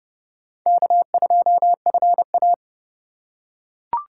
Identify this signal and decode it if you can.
Morse code, heard as a keyed tone, about 30 wpm, 700 Hz. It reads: K2FA
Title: USA Call Signs - 4 Characters - 3x Character Spacing - 30wpm